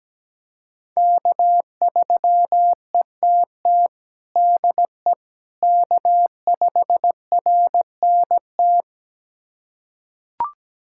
Morse code, heard as a keyed tone, about 17 wpm, 700 Hz. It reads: K3ETT DE K5RNT
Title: USA Callsign DE Callsign - 5 Characters - 17wpm